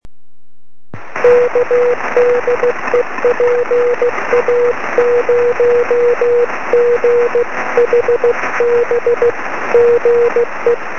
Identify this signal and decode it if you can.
Morse code, heard as a keyed tone, about 16 wpm, 470 Hz, with noise. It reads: K DE PA0GHB GE
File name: sm7lcb / 20 23cm / 1813 pa0ghb 23cm